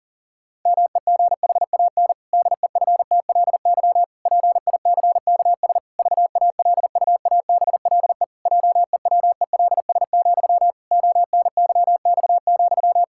Morse code, heard as a keyed tone, about 40 wpm, 700 Hz. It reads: MEGHAN DEFTLY PICKS VALUABLE JEWELS, ONYX,
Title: Pangram Challenge v2 - 40wpm